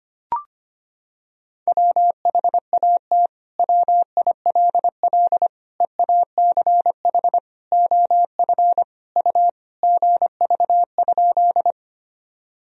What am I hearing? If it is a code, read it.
Morse code, heard as a keyed tone, about 25 wpm, 700 Hz. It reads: WHAT WILL EACH OF U GV?
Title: Sentences from Top 200 Words and Abbreviations 25wpm